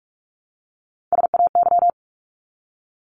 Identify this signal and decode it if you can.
Morse code, heard as a keyed tone, about 44 wpm, 700 Hz. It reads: SAY
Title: ES:S Suffix - Top 100 Words with Base Words 45wpm